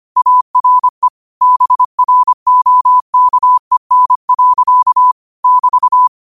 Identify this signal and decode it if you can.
Morse code, heard as a keyed tone, about 25 wpm, 1000 Hz. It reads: ARE BROKEN. <BT>
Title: 25 WPM morse code quotes for Thu, 29 Feb 2024